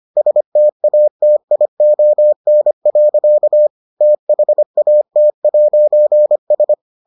Morse code, heard as a keyed tone, about 25 wpm, 600 Hz. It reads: STATION. THAT'S